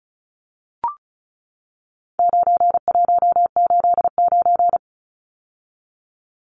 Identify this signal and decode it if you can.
Morse code, heard as a keyed tone, about 34 wpm, 700 Hz. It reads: 9189
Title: Numbers - 4-Digit 35wpm